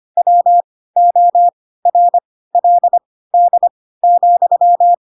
Morse code, heard as a keyed tone, about 25 wpm, spaced as for 16 wpm, 700 Hz. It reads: WORLD,
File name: Pinocchio / Pinocchio ch4 10 25 ik1hge